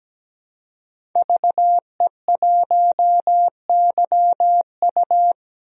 Morse code, heard as a keyed tone, about 17 wpm, 700 Hz. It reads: VE1YU
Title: DXCC-100 Call Signs - 5 Characters - 17wpm